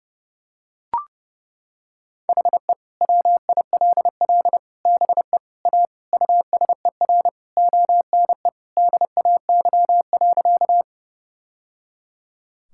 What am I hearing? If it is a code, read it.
Morse code, heard as a keyed tone, about 30 wpm, 700 Hz. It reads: HE WILL BE A USER ONE DAY.
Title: Sentences from Top 1000 Words - Review Part 5 - 30wpm